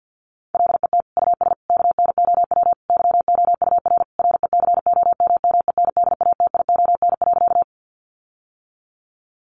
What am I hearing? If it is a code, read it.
Morse code, heard as a keyed tone, about 50 wpm, 700 Hz. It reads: LET US KNOW YOUR RECOMMENDATION.
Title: Sentences from Top 1400 Words 50wpm